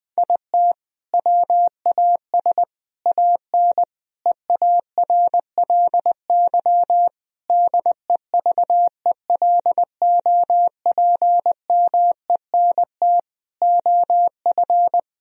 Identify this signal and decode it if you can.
Morse code, heard as a keyed tone, about 21 wpm, 700 Hz. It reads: IT WAS AN EARLY DEVELOPMENT OF
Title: Sentences from Top 1000 Words - Review Part 4 - 20wpm